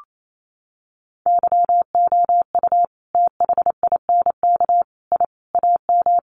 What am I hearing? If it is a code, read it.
Morse code, heard as a keyed tone, about 28 wpm, 700 Hz. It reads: YOU THINK I AM
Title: Sentences from Top 1200 Words 28wpm